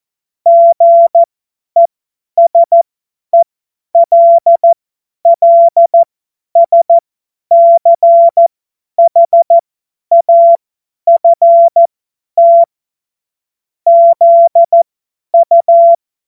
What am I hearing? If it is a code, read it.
Morse code, heard as a keyed tone, about 14 wpm, spaced as for 10 wpm, 680 Hz. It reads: GESELLSCHAFT ZU
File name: Morsekurs / Witze / Witz 29. Juni 2021, 50 F70